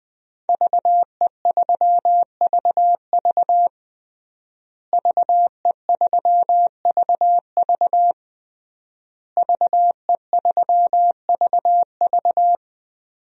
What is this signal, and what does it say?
Morse code, heard as a keyed tone, about 20 wpm, 700 Hz. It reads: VE3VV VE3VV VE3VV
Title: DXCC-100 Call Signs - 5 Characters - Repeated 3x - 20wpm